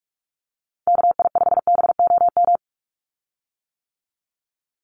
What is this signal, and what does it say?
Morse code, heard as a keyed tone, about 44 wpm, 700 Hz. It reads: KI5BOM
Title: USA Call Signs 45wpm